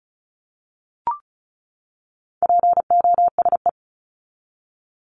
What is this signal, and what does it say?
Morse code, heard as a keyed tone, about 34 wpm, 700 Hz. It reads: POSE
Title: Top 500 Words 35wpm